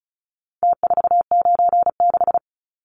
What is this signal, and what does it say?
Morse code, heard as a keyed tone, about 34 wpm, 700 Hz. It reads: T496
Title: Numbers - Contest Style - 4-Digit 35wpm